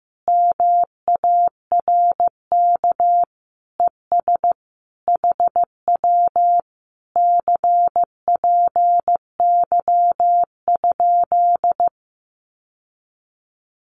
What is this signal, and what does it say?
Morse code, heard as a keyed tone, about 14 wpm, 700 Hz. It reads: MARK ES HW CPY?
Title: Common QSO Phrases Encore 15wpm